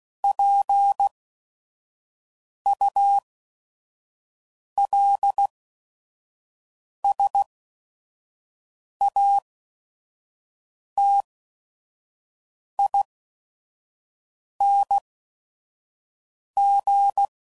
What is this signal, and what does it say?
Morse code, heard as a keyed tone, about 16 wpm, spaced as for 5 wpm, 800 Hz. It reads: PULSATING
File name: CW Code Groups @5WPM